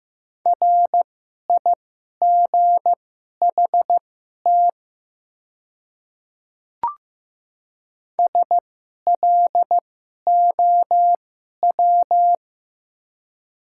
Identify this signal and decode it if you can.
Morse code, heard as a keyed tone, about 15 wpm, spaced as for 11 wpm, 700 Hz. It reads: RIGHT SLOW
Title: 2x Character Spacing - Top 500 Words - 15wpm